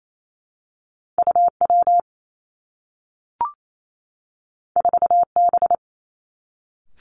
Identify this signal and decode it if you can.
Morse code, heard as a keyed tone, about 28 wpm, 700 Hz. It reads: U W 4 B
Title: Two Challenging Letter-Number Combinations 28wpm